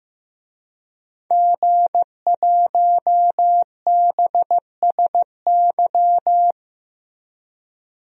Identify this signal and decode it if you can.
Morse code, heard as a keyed tone, about 15 wpm, 700 Hz. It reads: G1BSY
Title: DXCC-100 Call Signs - 5 Characters - Repeated 3x - 15wpm